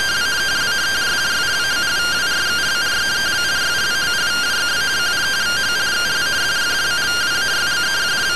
Two-tone teleprinter signal, about 170 Hz shift.